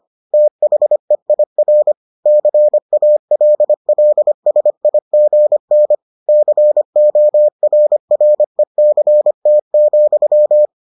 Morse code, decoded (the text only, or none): THEIR CALLSIGN CORRECT,